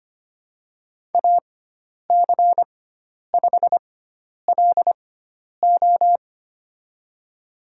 Morse code, heard as a keyed tone, about 25 wpm, spaced as for 10 wpm, 700 Hz. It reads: AC5LO
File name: Lesson 16g - Letter c - Call Signs 25wpm